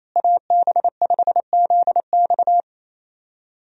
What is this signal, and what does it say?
Morse code, heard as a keyed tone, about 28 wpm, 700 Hz. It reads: AB5ZX
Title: USA Call Signs - 5 Characters - Repeated 3x - 28wpm